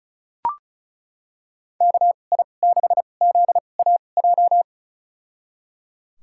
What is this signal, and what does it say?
Morse code, heard as a keyed tone, about 34 wpm, spaced as for 25 wpm, 700 Hz. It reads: KI6ZAJ
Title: USA Call Signs - 6 Characters - 2x Character Spacing - 35wpm